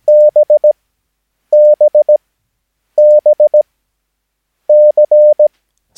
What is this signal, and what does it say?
Morse code, heard as a keyed tone, about 17 wpm, spaced as for 8 wpm, 600 Hz. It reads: BBBC
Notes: Övningsmaterial med alfabetet A till Ö samt siffror 0 till 9 enligt International/Continental Morse Code.